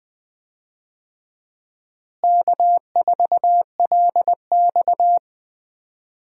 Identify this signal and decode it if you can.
Morse code, heard as a keyed tone, about 20 wpm, 700 Hz. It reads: K4LX
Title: USA Call Signs - 4 Characters - Repeated 3x - 20wpm